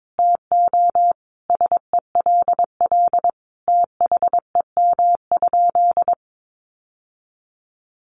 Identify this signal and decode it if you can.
Morse code, heard as a keyed tone, about 22 wpm, 700 Hz. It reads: TO SELL THEM?
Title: Sentences from Top 1100 Words 22wpm